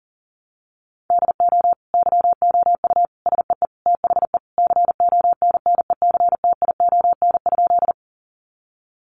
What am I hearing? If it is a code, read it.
Morse code, heard as a keyed tone, about 40 wpm, 700 Hz. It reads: DO YOU SEE THE CONNECTION?
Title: Sentences from Top 1300 Words 40wpm